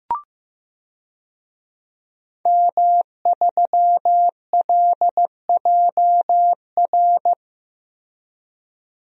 Morse code, heard as a keyed tone, about 15 wpm, 700 Hz. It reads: M3LJR
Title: DXCC-100 Call Signs - 5 Characters - Repeated 3x - 15wpm